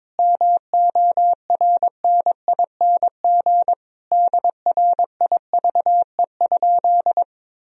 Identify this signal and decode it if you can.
Morse code, heard as a keyed tone, about 22 wpm, 700 Hz. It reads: MORNING DRIVE?
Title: Sentences from Top 1000 Words - Review Part 3 - 22wpm